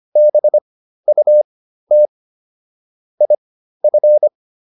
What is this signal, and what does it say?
Morse code, heard as a keyed tone, about 25 wpm, spaced as for 13 wpm, 600 Hz. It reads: BUT IF